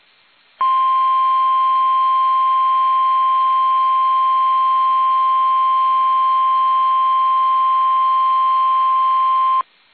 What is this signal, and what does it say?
Emergency Alert System (SAME) tones (the 1050 Hz attention tone). It is heard from 0.5 to 9.6 s.
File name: eas / KUSO / KUSO